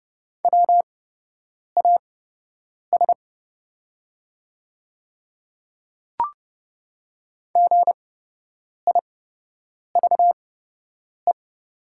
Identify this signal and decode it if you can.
Morse code, heard as a keyed tone, about 29 wpm, spaced as for 8 wpm, 690 Hz. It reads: WAS GIVE
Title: 8x Character Spacing - Top 200 Words - 30wpm